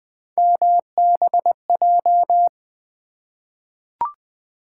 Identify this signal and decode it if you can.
Morse code, heard as a keyed tone, about 20 wpm, 700 Hz. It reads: MBJ
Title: Three Letter-Number Combinations - 20wpm